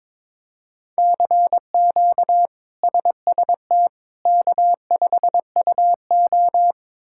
Morse code, heard as a keyed tone, about 22 wpm, 700 Hz. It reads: CQ SST K5UO
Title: K1USN SST QSO Practice - 22wpm